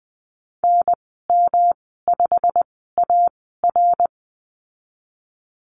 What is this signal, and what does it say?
Morse code, heard as a keyed tone, about 20 wpm, spaced as for 15 wpm, 700 Hz. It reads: NM5AR
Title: 147 Lesson 24g - Letter y - Call Signs 20wpm